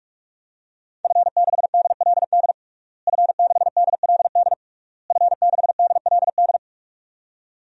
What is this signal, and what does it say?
Morse code, heard as a keyed tone, about 46 wpm, 700 Hz. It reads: F6DLD F6DLD F6DLD
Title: DXCC-100 Call Signs - 5 Characters - Repeated 3x - 45wpm